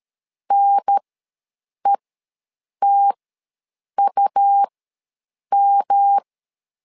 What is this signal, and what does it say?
Morse code, heard as a keyed tone, about 13 wpm, spaced as for 7 wpm, 800 Hz. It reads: NETUM
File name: LEZIONI / 3 MWRO / 3 M ws 12wpm 1745